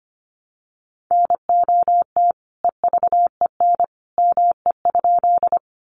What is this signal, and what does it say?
Morse code, heard as a keyed tone, about 25 wpm, 700 Hz. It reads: NOT EVEN ME?
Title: Sentences from Top 600 Words 25wpm